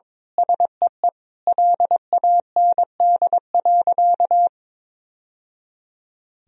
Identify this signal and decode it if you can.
Morse code, heard as a keyed tone, about 22 wpm, 700 Hz. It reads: SEE LAND.
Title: Sentences from Top 1000 Words - Review Part 2 - 22wpm